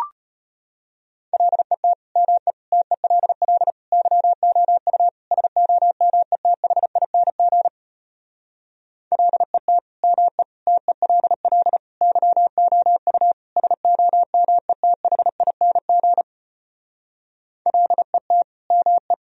Morse code, heard as a keyed tone, about 33 wpm, 700 Hz. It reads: LET ME TELL YOU SOMETHING LET ME TELL YOU SOMETHING LET ME